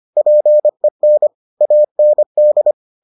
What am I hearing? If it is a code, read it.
Morse code, heard as a keyed tone, about 25 wpm, 600 Hz. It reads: PEN AND